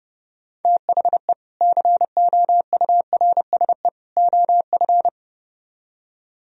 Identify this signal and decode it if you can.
Morse code, heard as a keyed tone, about 38 wpm, 700 Hz. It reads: THE COURSE OF